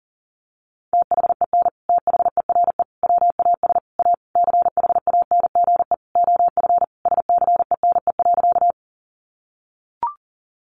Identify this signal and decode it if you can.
Morse code, heard as a keyed tone, about 40 wpm, 700 Hz. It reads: THEN THERE WAS A CHANGE OF SCENE.
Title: Sentences from Top 1100 Words 40wpm